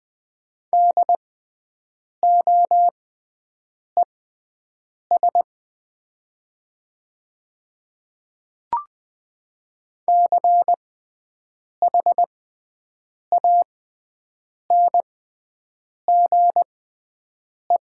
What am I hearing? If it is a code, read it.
Morse code, heard as a keyed tone, about 20 wpm, spaced as for 7 wpm, 700 Hz. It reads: DOES CHANGE
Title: 6x Character Spacing - Top 200 Words - 20wpm